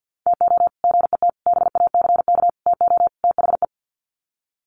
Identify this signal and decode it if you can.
Morse code, heard as a keyed tone, about 48 wpm, 700 Hz. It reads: TO GET BACK TO THE
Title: Sets of 5 Words Encore 50wpm